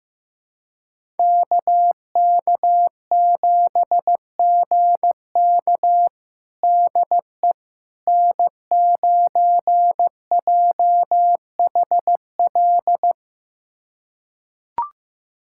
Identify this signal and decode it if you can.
Morse code, heard as a keyed tone, about 15 wpm, 700 Hz. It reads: KK7GK DE N9JHL
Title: USA Callsign DE Callsign - 5 Characters - 15wpm